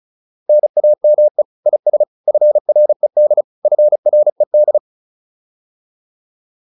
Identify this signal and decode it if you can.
Morse code, heard as a keyed tone, about 34 wpm, 600 Hz. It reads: NAME IS FRED FRED
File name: QSO404 35